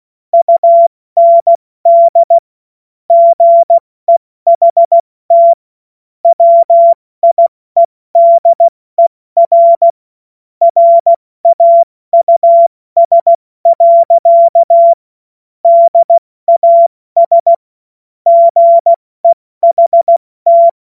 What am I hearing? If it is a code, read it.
Morse code, heard as a keyed tone, about 16 wpm, 680 Hz. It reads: UND GEHT WIEDER RAUS. DAS GEHT
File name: Morsekurs / Witze / Witz 02 Januar 2024 70 F80